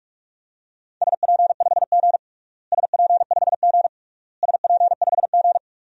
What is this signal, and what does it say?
Morse code, heard as a keyed tone, about 48 wpm, 700 Hz. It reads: SP5G SP5G SP5G
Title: DXCC-100 Call Signs - 4 Characters - Repeated 3x - 45wpm